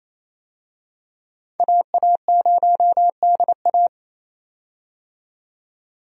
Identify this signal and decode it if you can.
Morse code, heard as a keyed tone, about 28 wpm, 700 Hz. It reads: AA0DA
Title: USA Call Signs - 5 Characters - 28wpm